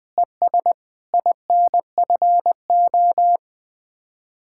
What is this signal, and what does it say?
Morse code, heard as a keyed tone, about 20 wpm, 700 Hz. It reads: ES INFO
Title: Speed-Racing Common QSO Phrases Encore - 20wpm